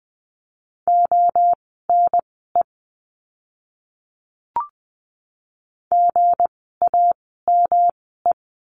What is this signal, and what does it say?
Morse code, heard as a keyed tone, about 20 wpm, spaced as for 15 wpm, 700 Hz. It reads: ONE GAME
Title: 160 Lesson 26f - Letter g - Words 20wpm